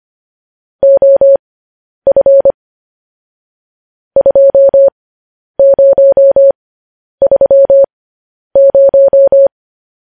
Morse code, heard as a keyed tone, about 25 wpm, spaced as for 10 wpm, 550 Hz. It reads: OF 2030
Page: news headlines in morse code at 10 wpm